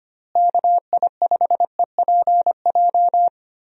Morse code, heard as a keyed tone, about 25 wpm, 700 Hz. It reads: KI5EPJ
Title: USA Call Signs - 6 Characters - Repeated 2x - 25wpm